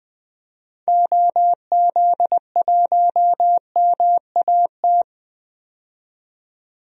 Morse code, heard as a keyed tone, about 20 wpm, 700 Hz. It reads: OZ1MAT
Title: DXCC-100 Call Signs - 6 Characters - 20wpm